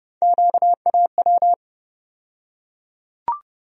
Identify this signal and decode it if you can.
Morse code, heard as a keyed tone, about 30 wpm, 700 Hz. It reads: QAW
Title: USA Call Sign Postfixes - 3 Letters 30wpm